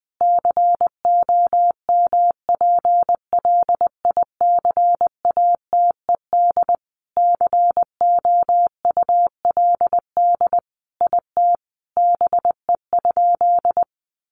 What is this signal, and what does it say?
Morse code, heard as a keyed tone, about 20 wpm, 700 Hz. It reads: COMPLICATED COULD IT BE?
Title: Sentences from Top 1200 Words 20wpm